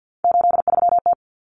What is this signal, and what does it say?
Morse code, heard as a keyed tone, about 50 wpm, 700 Hz. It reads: 83T
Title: Common Cut Numbers - 3-Digit 50wpm